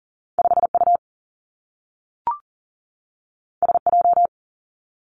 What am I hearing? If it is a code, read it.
Morse code, heard as a keyed tone, about 41 wpm, 700 Hz. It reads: 5 U S J
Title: Two Challenging Letter-Number Combinations 40wpm